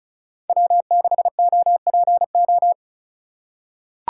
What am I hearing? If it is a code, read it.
Morse code, heard as a keyed tone, about 34 wpm, 700 Hz. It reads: W6OPO
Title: K1USN SST QSO Practice - 35wpm